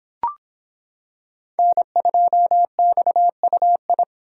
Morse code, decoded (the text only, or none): N2XUI